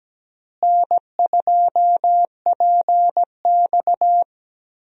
Morse code, heard as a keyed tone, about 17 wpm, 700 Hz. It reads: N2PX
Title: USA Call Signs - 4 Characters - Repeated 3x - 17wpm